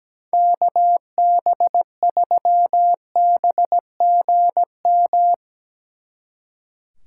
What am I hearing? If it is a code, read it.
Morse code, heard as a keyed tone, about 18 wpm, 700 Hz. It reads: KB3BGM
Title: Speed Racing - USA 2x3 Call Signs 17wpm